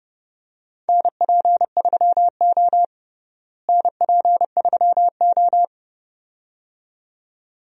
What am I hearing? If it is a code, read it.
Morse code, heard as a keyed tone, about 30 wpm, 700 Hz. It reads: NP3O NP3O
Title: USA Call Signs - 4 Characters - Repeated 2x - 30wpm